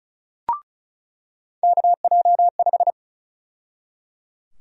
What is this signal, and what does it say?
Morse code, heard as a keyed tone, about 34 wpm, 700 Hz. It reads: KJ5
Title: USA Call Sign Prefixes - 35wpm